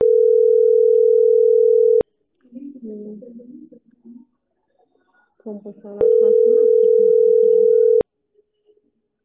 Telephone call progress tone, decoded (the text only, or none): ringback tone